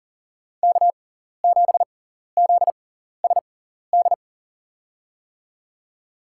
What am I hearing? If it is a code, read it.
Morse code, heard as a keyed tone, about 40 wpm, spaced as for 14 wpm, 700 Hz. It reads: K7ZSD
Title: USA Call Signs - 6x Character Spacing - 40wpm